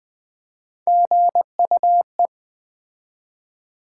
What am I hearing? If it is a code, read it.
Morse code, heard as a keyed tone, about 20 wpm, 700 Hz. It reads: GUE
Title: Build A Word - 3 Letters At A Time - 9-Letter Words - 20wpm